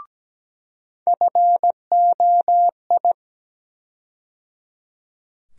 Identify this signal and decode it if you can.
Morse code, heard as a keyed tone, about 17 wpm, 700 Hz. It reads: FOI
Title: USA Call Sign Postfixes - 3 Letters 17wpm